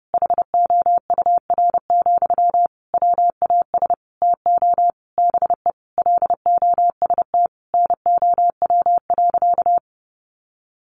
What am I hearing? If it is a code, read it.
Morse code, heard as a keyed tone, about 30 wpm, 700 Hz. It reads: HOUR, WAS TO BE LOST NOW.
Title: Sentences from Top 1400 Words 30wpm